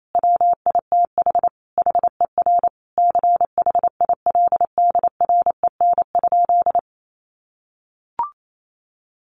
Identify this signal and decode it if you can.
Morse code, heard as a keyed tone, about 28 wpm, 700 Hz. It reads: WITH HER CHILDREN?
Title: Sentences from Top 1500 Words 28wpm